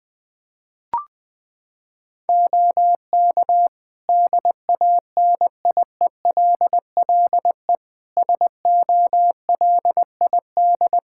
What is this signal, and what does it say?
Morse code, heard as a keyed tone, about 20 wpm, 700 Hz. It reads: OK DANIELLE SOLID
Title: Practice QSO - Exchange 3 - More Info - 20wpm